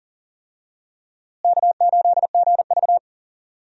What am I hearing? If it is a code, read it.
Morse code, heard as a keyed tone, about 38 wpm, 700 Hz. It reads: K8GV
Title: USA Call Signs - 4 Characters - Repeated 2x - 40wpm